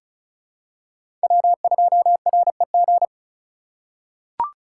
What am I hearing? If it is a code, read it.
Morse code, heard as a keyed tone, about 35 wpm, 700 Hz. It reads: W2REG
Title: USA Call Signs - 3x Character Spacing - 35wpm